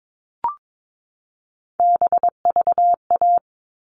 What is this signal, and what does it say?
Morse code, heard as a keyed tone, about 22 wpm, 700 Hz. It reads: BVA
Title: All Cut Numbers - 3-Digit 22wpm